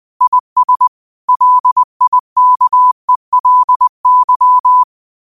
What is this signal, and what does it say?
Morse code, heard as a keyed tone, about 20 wpm, 1000 Hz. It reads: IS LIKELY
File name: qotdfiles / 20wpm